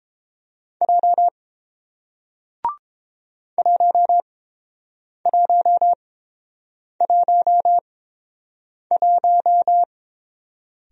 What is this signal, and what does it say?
Morse code, heard as a keyed tone, about 30 wpm, spaced as for 7 wpm, 700 Hz. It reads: J 1111